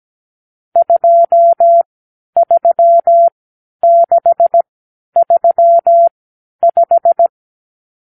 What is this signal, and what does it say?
Morse code, heard as a keyed tone, about 17 wpm, spaced as for 10 wpm, 680 Hz. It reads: 23635
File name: k7qo / Lesson 50 (1-6)